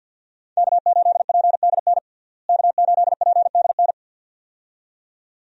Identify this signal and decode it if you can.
Morse code, heard as a keyed tone, about 48 wpm, 700 Hz. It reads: K8PDN K8PDN
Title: USA Call Signs - 5 Characters - Repeated 2x - 50wpm